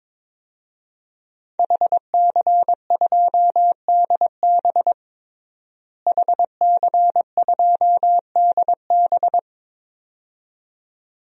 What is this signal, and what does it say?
Morse code, heard as a keyed tone, about 22 wpm, 700 Hz. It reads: HC2DB HC2DB
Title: DXCC-100 Call Signs - 5 Characters - Repeated 2x - 22wpm